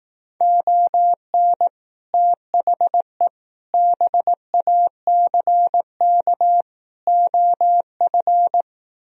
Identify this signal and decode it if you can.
Morse code, heard as a keyed tone, about 18 wpm, 700 Hz. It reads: ON THE BACK OF
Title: Speed-Racing - Sets of 4 Words 15wpm